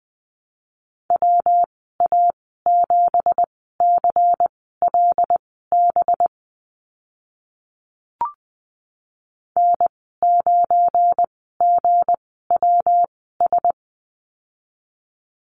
Morse code, heard as a keyed tone, about 20 wpm, spaced as for 14 wpm, 700 Hz. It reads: WA7CLB N9GWS
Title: 193 Lesson 31g - Letter v - Call Signs 20wpm